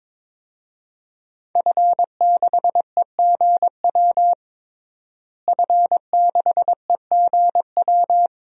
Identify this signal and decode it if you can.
Morse code, heard as a keyed tone, about 22 wpm, 700 Hz. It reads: F6EGW F6EGW
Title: DXCC-100 Call Signs - 5 Characters - Repeated 3x - 22wpm